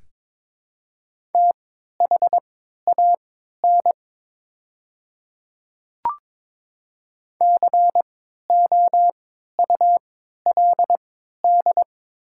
Morse code, heard as a keyed tone, about 22 wpm, spaced as for 13 wpm, 700 Hz. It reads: THAN COULD
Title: 3x Character Spacing - Top 100 Words - 22wpm